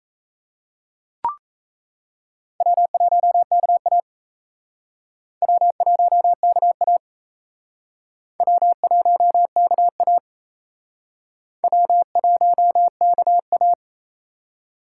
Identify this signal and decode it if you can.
Morse code, heard as a keyed tone, about 34 wpm, 700 Hz. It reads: W1KA W1KA W1KA W1KA
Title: Speed Racing - USA Call Signs 28wpm